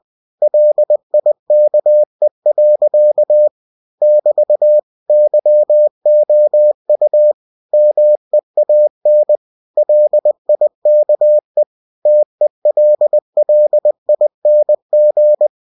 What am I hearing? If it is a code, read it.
Morse code, heard as a keyed tone, about 20 wpm, 600 Hz. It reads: LIKE. <BT> YOU MEAN LIKE TELLING
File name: cw-books / winnie-the-pooh / winnie-20wpm-page-0065